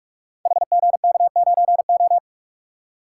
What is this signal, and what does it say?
Morse code, heard as a keyed tone, about 44 wpm, 700 Hz. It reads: HGK9O
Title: Code Groups - 5 Letter:Numbers - 45wpm